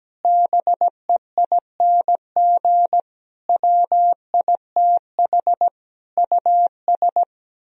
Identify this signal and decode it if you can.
Morse code, heard as a keyed tone, about 17 wpm, 700 Hz. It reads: BEING WITH US
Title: Speed-Racing - Sets of 5 Words 17wpm